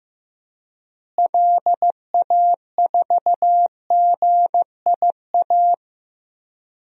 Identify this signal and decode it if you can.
Morse code, heard as a keyed tone, about 16 wpm, 700 Hz. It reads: LA4GIA
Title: DXCC-100 Call Signs - 6 Characters - Repeated 2x - 15wpm